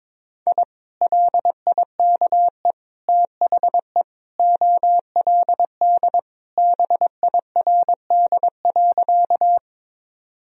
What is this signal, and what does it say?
Morse code, heard as a keyed tone, about 22 wpm, 700 Hz. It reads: I LIKE THE OLD BIRD.
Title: Sentences from Top 1000 Words - Review Part 2 - 22wpm